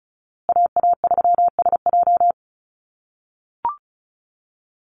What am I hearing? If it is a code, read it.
Morse code, heard as a keyed tone, about 34 wpm, 700 Hz. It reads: AA3SJ
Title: USA Call Signs Repeated 3x 35wpm